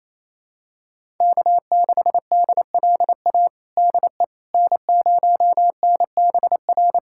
Morse code, heard as a keyed tone, about 27 wpm, 700 Hz. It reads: K6DLA DE N0NBR
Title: USA Callsign DE Callsign - 5 Characters - 28wpm